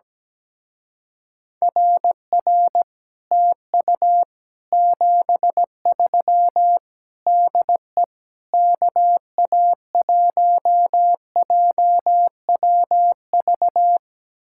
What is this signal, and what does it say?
Morse code, heard as a keyed tone, about 17 wpm, 700 Hz. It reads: RR TU 73 DE KA1JWV
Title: Speed-Racing Common QSO Phrases - 17wpm